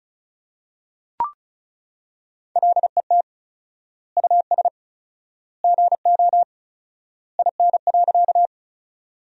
Morse code, heard as a keyed tone, about 34 wpm, 700 Hz. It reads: LET US GO IN.
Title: 4x Word-Spacing - Sentences from Top 300 Words - 35wpm